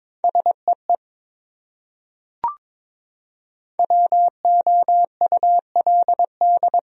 Morse code, heard as a keyed tone, about 22 wpm, 700 Hz. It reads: SEE WOULD
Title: Top 100 Words Repeated 22wpm